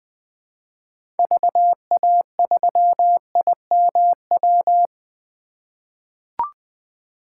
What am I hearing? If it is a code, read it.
Morse code, heard as a keyed tone, about 20 wpm, 700 Hz. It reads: VA3IMW
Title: 2023 POTA Hunter Call Signs - 20wpm